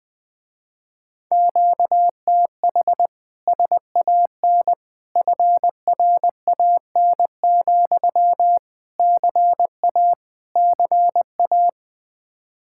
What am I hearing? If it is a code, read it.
Morse code, heard as a keyed tone, about 20 wpm, 700 Hz. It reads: QTH SAN FRAN, CA CA